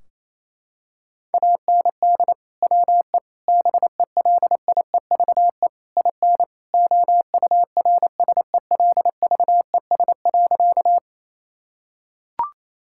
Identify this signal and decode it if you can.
Morse code, heard as a keyed tone, about 27 wpm, 700 Hz. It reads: AND WE BELIEVE IN OURSELVES.
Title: Sentences from Top 1600 Words 28wpm